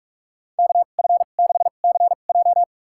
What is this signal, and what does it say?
Morse code, heard as a keyed tone, about 46 wpm, spaced as for 33 wpm, 700 Hz. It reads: KF6CJ
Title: USA Call Signs - 2x Character Spacing - 45wpm